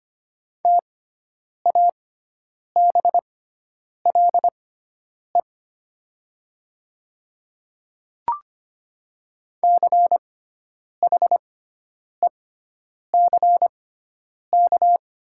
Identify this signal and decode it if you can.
Morse code, heard as a keyed tone, about 25 wpm, spaced as for 9 wpm, 700 Hz. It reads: TABLE CHECK
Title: 6x Character Spacing - Top 500 Words - 25wpm